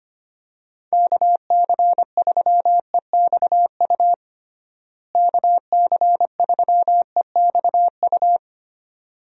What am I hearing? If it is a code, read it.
Morse code, heard as a keyed tone, about 25 wpm, 700 Hz. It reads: KC3EXU KC3EXU
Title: USA Call Signs - 6 Characters - Repeated 2x - 25wpm